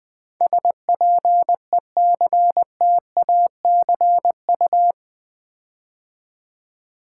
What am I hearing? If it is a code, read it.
Morse code, heard as a keyed tone, about 20 wpm, 700 Hz. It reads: SPECTACU
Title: Build A Word - 4 Letters At A Time - 11-Letter Words - 20wpm